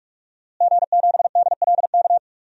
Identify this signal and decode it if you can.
Morse code, heard as a keyed tone, about 46 wpm, 700 Hz. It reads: G7DLK
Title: DXCC-100 Call Signs - 5 Characters - Repeated 2x - 45wpm